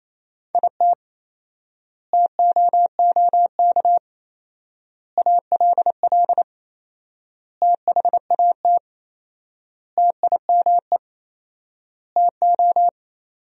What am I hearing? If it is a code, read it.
Morse code, heard as a keyed tone, about 28 wpm, 700 Hz. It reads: IT TOOK ALL THAT TIME TO
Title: 4x Word-Spacing - Sentences from Top 300 Words - 28wpm